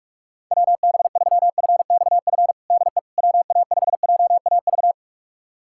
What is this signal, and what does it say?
Morse code, heard as a keyed tone, about 46 wpm, 690 Hz. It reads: WB3FXF DE WA5JAV
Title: USA Callsign DE Callsign 45wpm